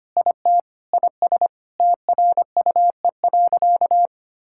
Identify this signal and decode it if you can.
Morse code, heard as a keyed tone, about 25 wpm, 700 Hz. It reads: IT IS TRUE.
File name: practice / 25wpm / Sentences with 4-Word Phrases - 25wpm